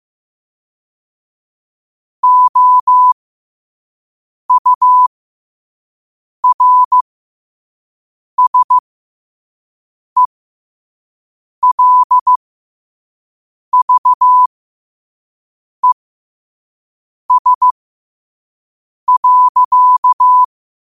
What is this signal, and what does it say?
Morse code, heard as a keyed tone, about 15 wpm, spaced as for 5 wpm, 1000 Hz. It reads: OURSELVES.
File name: qotdfiles / 5wpm